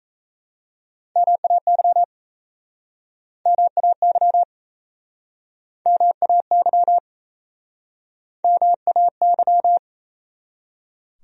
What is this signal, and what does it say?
Morse code, heard as a keyed tone, about 37 wpm, 700 Hz. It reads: MAY MAY MAY MAY